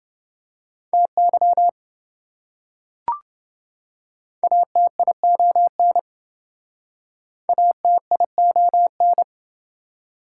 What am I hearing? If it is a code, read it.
Morse code, heard as a keyed tone, about 27 wpm, 700 Hz. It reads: TY ATION ATION